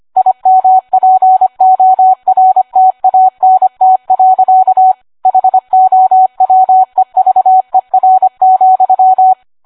Morse code, heard as a keyed tone, about 25 wpm, 750 Hz. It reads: IMPORTANT. HOWEVER,